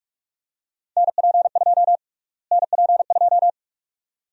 Morse code, heard as a keyed tone, about 42 wpm, 700 Hz. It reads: NP2 NP2
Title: USA Call Sign Prefixes - Repeated 2x - 45wpm